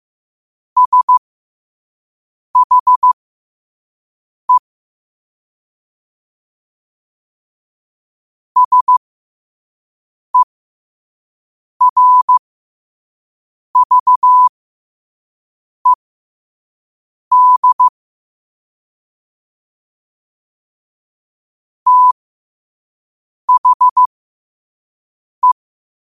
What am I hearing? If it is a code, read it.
Morse code, heard as a keyed tone, about 15 wpm, spaced as for 5 wpm, 1000 Hz. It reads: SHE SERVED THE
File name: qotdfiles / 5wpm